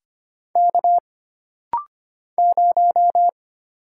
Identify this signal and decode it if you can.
Morse code, heard as a keyed tone, about 25 wpm, 700 Hz. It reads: K 0
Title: Single Letter-Number - Rapid-Fire 25wpm